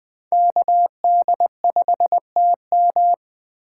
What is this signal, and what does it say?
Morse code, heard as a keyed tone, about 20 wpm, 700 Hz. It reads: KD5TM